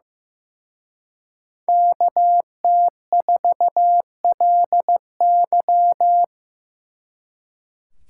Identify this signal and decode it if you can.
Morse code, heard as a keyed tone, about 17 wpm, 700 Hz. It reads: KT4LY
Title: Speed Racing - USA 2x2 Call Signs 15wpm